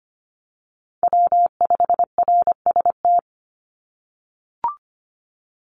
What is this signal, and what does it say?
Morse code, heard as a keyed tone, about 25 wpm, 700 Hz. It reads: W5RST
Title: NA SOTA Chaser Callsigns 25wpm